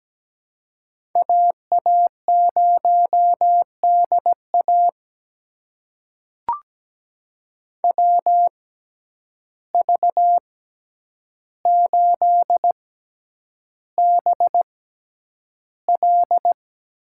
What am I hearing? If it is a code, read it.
Morse code, heard as a keyed tone, about 17 wpm, spaced as for 9 wpm, 700 Hz. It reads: AA0DA WV8BL
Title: USA Call Signs - 5 Characters - 6x Character Spacing - 17wpm